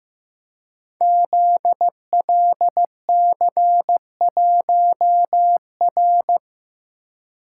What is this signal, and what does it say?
Morse code, heard as a keyed tone, about 15 wpm, 700 Hz. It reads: ZLC1R
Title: Code Groups - 5 Letter:Numbers - 15wpm